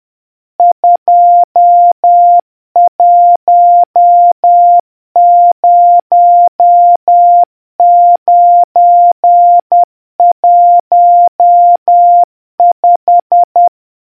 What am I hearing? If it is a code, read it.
Morse code, heard as a keyed tone, about 10 wpm, 700 Hz. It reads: 210915